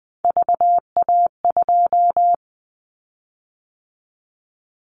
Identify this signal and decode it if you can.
Morse code, heard as a keyed tone, about 20 wpm, 700 Hz. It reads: VA2
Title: Callsign Prefixes for Top 25 DXCC - 20wpm